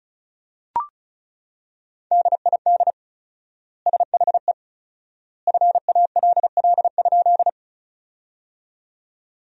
Morse code, heard as a keyed tone, about 36 wpm, 700 Hz. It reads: DID SHE FALL?
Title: 4x Word-Spacing - Sentences from Top 500 Words - 35wpm